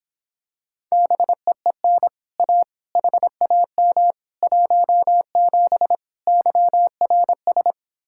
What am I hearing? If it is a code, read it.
Morse code, heard as a keyed tone, about 26 wpm, 700 Hz. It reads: BEEN A HAM 17 YRS